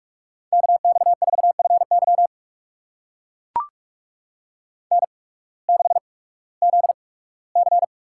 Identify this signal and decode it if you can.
Morse code, heard as a keyed tone, about 44 wpm, spaced as for 21 wpm, 700 Hz. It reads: KX4FY N6ZC
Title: USA Call Signs - 5 Characters - 8x Character Spacing - 45wpm